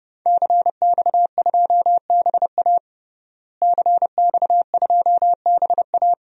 Morse code, heard as a keyed tone, about 31 wpm, 700 Hz. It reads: CX2BA CX2BA
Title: DXCC-100 Call Signs - 5 Characters - Repeated 2x - 30wpm